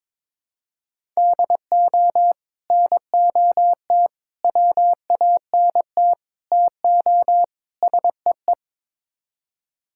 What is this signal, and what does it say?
Morse code, heard as a keyed tone, about 22 wpm, 700 Hz. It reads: DO NOT WANT TO SEE
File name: practice / 22wpm / Build A Phrase - 5-Word Phrases - 22wpm